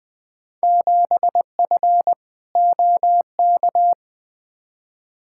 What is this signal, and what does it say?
Morse code, heard as a keyed tone, about 20 wpm, 700 Hz. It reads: 7F OK
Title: Field Day Exchanges Encore - 20wpm